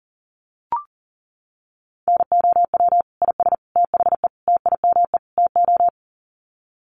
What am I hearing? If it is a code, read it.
Morse code, heard as a keyed tone, about 40 wpm, 700 Hz. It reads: NOW IS THE TIME TO
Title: Sets of 5 Words Encore2 40wpm